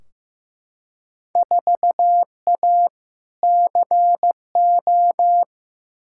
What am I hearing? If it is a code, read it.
Morse code, heard as a keyed tone, about 15 wpm, 700 Hz. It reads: 4A CO
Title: Field Day Exchanges - 15wpm